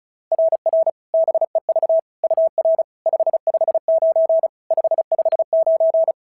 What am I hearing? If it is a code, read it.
Morse code, heard as a keyed tone, about 36 wpm, 690 Hz. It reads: RR BEV UR 559 559
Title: SOTA QSO Practice 35wpm